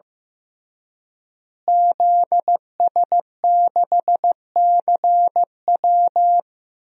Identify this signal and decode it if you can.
Morse code, heard as a keyed tone, about 15 wpm, 700 Hz. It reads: ZS6CW
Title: DXCC-100 Call Signs - 5 Characters - Repeated 2x - 15wpm